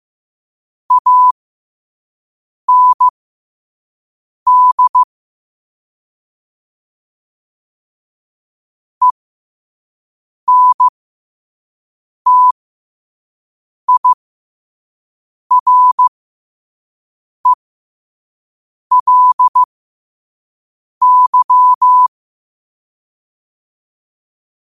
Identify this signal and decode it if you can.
Morse code, heard as a keyed tone, about 15 wpm, spaced as for 5 wpm, 1000 Hz. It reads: AND ENTIRELY
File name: qotdfiles / 5wpm